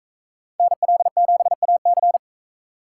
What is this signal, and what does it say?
Morse code, heard as a keyed tone, about 40 wpm, 700 Hz. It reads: NL7AC